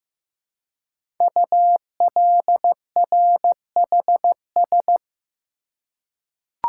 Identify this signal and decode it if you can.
Morse code, heard as a keyed tone, about 15 wpm, 700 Hz. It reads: ULRHS
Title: Code Groups - 5 Letters - 15wpm